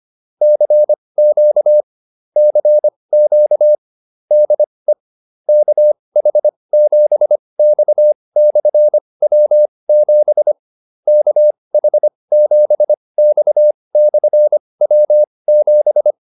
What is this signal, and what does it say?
Morse code, heard as a keyed tone, about 25 wpm, spaced as for 20 wpm, 600 Hz. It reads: CQ CQ DE KH7X/W7 KH7X/W7